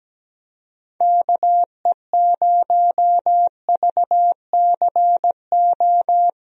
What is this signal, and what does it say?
Morse code, heard as a keyed tone, about 17 wpm, 700 Hz. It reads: KE0VCO
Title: USA Call Signs - 6 Characters - Repeated 2x - 17wpm